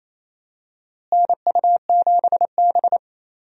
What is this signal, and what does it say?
Morse code, heard as a keyed tone, about 28 wpm, 700 Hz. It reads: NU7B
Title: USA Call Signs - 4 Characters - Repeated 3x - 28wpm